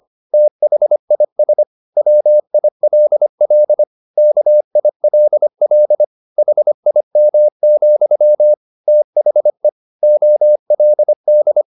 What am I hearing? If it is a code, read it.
Morse code, heard as a keyed tone, about 25 wpm, 600 Hz. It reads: THIS WILL KILL HIM, THE OLD
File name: the old man and the sea part 1 25wpm